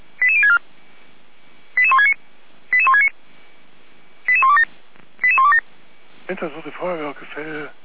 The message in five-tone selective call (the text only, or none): 88064 88169 88169 88127 88127